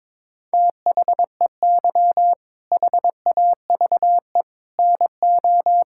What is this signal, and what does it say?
Morse code, heard as a keyed tone, about 22 wpm, 700 Hz. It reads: THEY HAVE NO
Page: morse-code-ninja-practice-22wpm